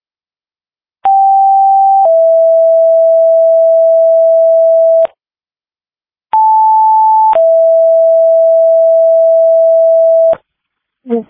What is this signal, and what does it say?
Two-tone page, heard as a one-second tone, then a three-second tone, about 780 then 660 Hz, about 880 then 660 Hz, with speech. It is heard from 1.1 to 10.3 s.